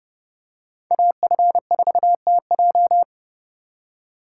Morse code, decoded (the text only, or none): AF4TJ